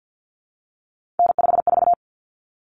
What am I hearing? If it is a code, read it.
Morse code, heard as a keyed tone, about 50 wpm, 700 Hz. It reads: N54
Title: Common Cut Numbers - 3-Digit 50wpm